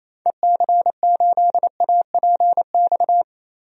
Morse code, heard as a keyed tone, about 28 wpm, 700 Hz. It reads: EC8APX
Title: DXCC-100 Call Signs - 6 Characters - Repeated 3x - 28wpm